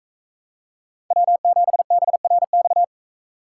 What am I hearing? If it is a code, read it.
Morse code, heard as a keyed tone, about 37 wpm, 700 Hz. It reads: W7BRX